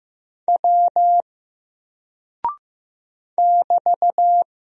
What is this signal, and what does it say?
Morse code, heard as a keyed tone, about 15 wpm, 700 Hz. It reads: W <BT>
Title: Single Character (Letter:Number:Punctuation:Prosign) 15wpm